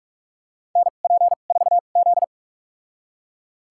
Morse code, heard as a keyed tone, about 46 wpm, spaced as for 32 wpm, 700 Hz. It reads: NP4Z
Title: USA Call Signs - 2x Character Spacing - 45wpm